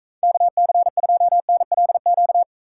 Morse code, heard as a keyed tone, about 45 wpm, 700 Hz. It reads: KC2NLQ